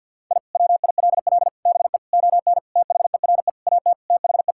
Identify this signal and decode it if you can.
Morse code, heard as a keyed tone, about 50 wpm, 700 Hz. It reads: I WILL BE ON THERE AT THE
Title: Sentences from Top 1000 Words - Review Part 5 - 50wpm